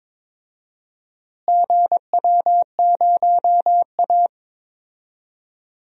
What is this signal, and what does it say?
Morse code, heard as a keyed tone, about 23 wpm, 700 Hz. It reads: GW0A
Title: DXCC-100 Call Signs - 4 Characters - Repeated 2x - 22wpm